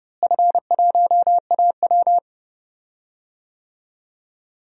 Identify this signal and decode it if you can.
Morse code, heard as a keyed tone, about 30 wpm, 700 Hz. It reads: F1AW
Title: DXCC-100 Call Signs - 4 Characters - Repeated 2x - 30wpm